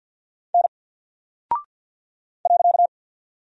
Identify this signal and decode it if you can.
Morse code, heard as a keyed tone, about 48 wpm, 700 Hz. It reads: N .
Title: Single Character (Letter:Number:Punctuation:Prosign) Alt 50wpm